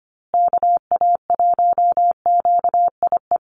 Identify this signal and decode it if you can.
Morse code, heard as a keyed tone, about 25 wpm, 700 Hz. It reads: KA1QIE
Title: USA Call Signs Repeated 3x 25wpm